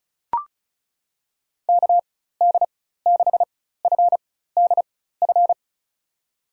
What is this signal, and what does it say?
Morse code, heard as a keyed tone, about 36 wpm, spaced as for 16 wpm, 700 Hz. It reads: KD6FDF
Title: USA Call Signs - 6 Characters - 4x Character Spacing - 35wpm